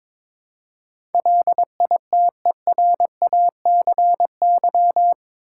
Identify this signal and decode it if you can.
Morse code, heard as a keyed tone, about 22 wpm, 700 Hz. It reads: LITERACY
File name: practice / 22wpm / Eight Letter Words - 22wpm